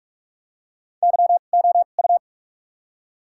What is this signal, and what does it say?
Morse code, heard as a keyed tone, about 48 wpm, spaced as for 33 wpm, 700 Hz. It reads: YOU
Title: 2x Character Spacing - Top 100 Words - 45wpm